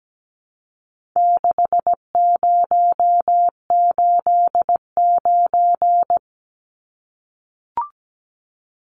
Morse code, heard as a keyed tone, about 17 wpm, 700 Hz. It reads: 6089
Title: Numbers - 4-Digit 17wpm